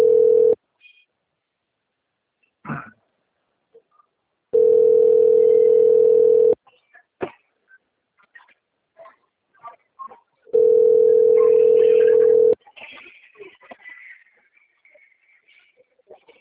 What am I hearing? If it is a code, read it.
Ringback tone (North America), 440 + 480 Hz.